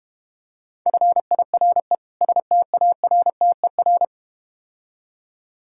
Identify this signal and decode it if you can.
Morse code, heard as a keyed tone, about 33 wpm, 700 Hz. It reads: FIRE STARTER